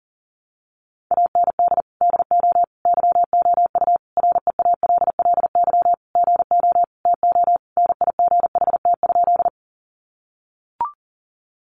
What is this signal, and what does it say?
Morse code, heard as a keyed tone, about 40 wpm, 700 Hz. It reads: AND DO YOU REALLY GO TO NIGHT?
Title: Sentences from Top 600 Words 40wpm